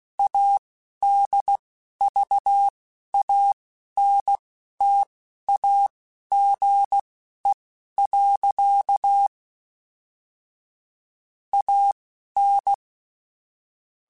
Morse code, heard as a keyed tone, about 16 wpm, spaced as for 11 wpm, 800 Hz. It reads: ADVANTAGE. AN
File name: Technical Subjects / Technical Subjects-mp3s / Sw Pwr Sup @10